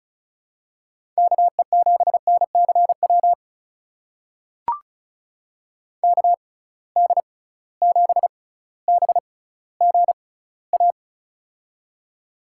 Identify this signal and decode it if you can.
Morse code, heard as a keyed tone, about 36 wpm, spaced as for 18 wpm, 700 Hz. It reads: KE7NCW KD7BGA
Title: USA Call Signs - 6 Characters - 6x Character Spacing - 35wpm